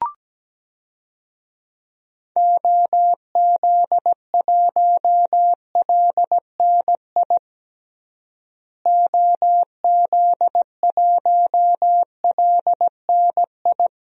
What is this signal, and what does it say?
Morse code, heard as a keyed tone, about 17 wpm, 700 Hz. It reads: OZ1LNI OZ1LNI
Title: DXCC-100 Call Signs - 6 Characters - Repeated 3x - 17wpm